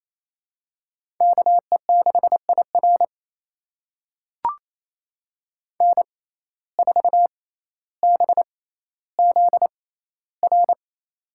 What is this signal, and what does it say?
Morse code, heard as a keyed tone, about 27 wpm, spaced as for 14 wpm, 700 Hz. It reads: KE6IR N4BZR
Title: USA Call Signs - 5 Characters - 6x Character Spacing - 28wpm